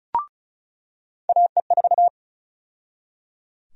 Morse code, heard as a keyed tone, about 36 wpm, 700 Hz. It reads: AE4
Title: USA Call Sign Prefixes - 35wpm